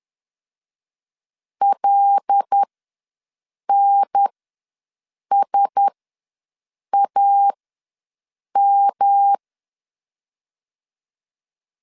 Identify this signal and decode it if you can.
Morse code, heard as a keyed tone, about 10 wpm, spaced as for 6 wpm, 800 Hz. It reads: LNSAM
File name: LEZIONI / 5 LBVC / 5 LBV ws 10wpm 3645